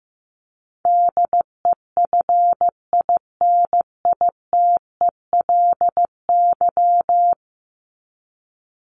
Morse code, heard as a keyed tone, about 15 wpm, 700 Hz. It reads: DEFINITELY
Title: Top 1751-2000 Words 15wpm